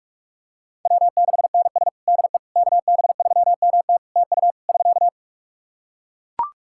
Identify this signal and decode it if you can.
Morse code, heard as a keyed tone, about 41 wpm, 700 Hz. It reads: W6NS DE KB3MT TU <SK>
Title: Practice QSO - Exchange 4 - Ending - 45wpm